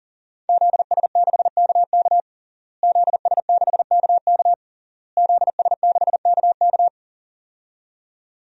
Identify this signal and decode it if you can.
Morse code, heard as a keyed tone, about 40 wpm, 700 Hz. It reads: ZS6KK ZS6KK ZS6KK
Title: DXCC-100 Call Signs - 5 Characters - Repeated 3x - 40wpm